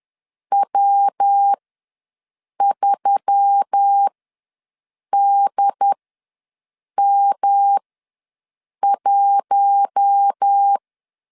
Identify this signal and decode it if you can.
Morse code, heard as a keyed tone, about 10 wpm, spaced as for 6 wpm, 800 Hz. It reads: W3DM1
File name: LEZIONI / 5 LBVC / 5 LBVC ws 10wpm 3733